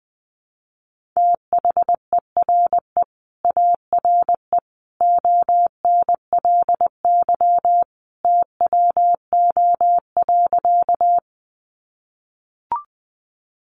Morse code, heard as a keyed tone, about 20 wpm, 700 Hz. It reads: THERE ARE ONLY TWO.
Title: Sentences from Top 200 Words 20wpm